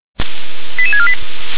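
Five-tone selective call (ZVEI), heard as quick runs of five tones, tones of about 70 ms. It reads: 88648
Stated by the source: ZVEI five-tone selective call received on an SDR.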